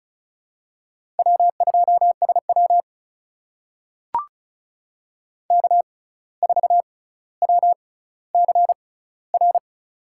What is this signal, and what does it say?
Morse code, heard as a keyed tone, about 36 wpm, spaced as for 17 wpm, 700 Hz. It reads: W2SW K4WCR
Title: USA Call Signs - 6x Character Spacing - 35wpm